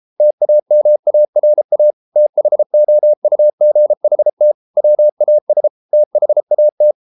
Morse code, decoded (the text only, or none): TAMARA THOUGHT WAS THAT